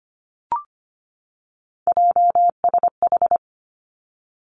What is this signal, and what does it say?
Morse code, heard as a keyed tone, about 25 wpm, 700 Hz. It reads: JSH
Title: Three Challenging Letter-Number Combinations 25wpm